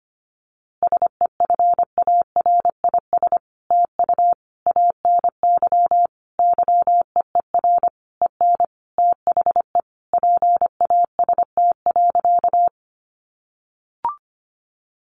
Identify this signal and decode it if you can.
Morse code, heard as a keyed tone, about 25 wpm, 700 Hz. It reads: SEFARIS TU ANY YEER EN THE PAST.
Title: A Sound Of Thunder 25wpm